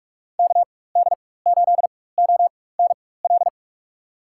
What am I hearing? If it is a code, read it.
Morse code, heard as a keyed tone, about 45 wpm, spaced as for 21 wpm, 700 Hz. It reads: KD8ONL
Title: USA Call Signs - 4x Character Spacing - 45wpm